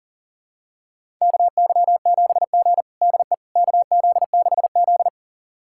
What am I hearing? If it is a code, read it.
Morse code, heard as a keyed tone, about 40 wpm, 700 Hz. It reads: KY7G DE KZ6Z
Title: USA Callsign DE Callsign - 4 Characters - 40wpm